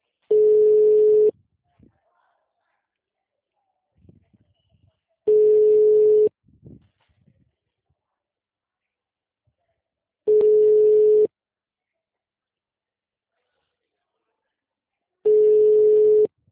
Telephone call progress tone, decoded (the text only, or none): ringback tone